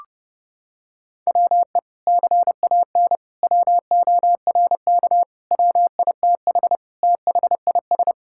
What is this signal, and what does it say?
Morse code, heard as a keyed tone, about 30 wpm, 700 Hz. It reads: WE CAN WORK WITH THIS
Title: Sentences from Top 1000 Words - Review Part 5 - 30wpm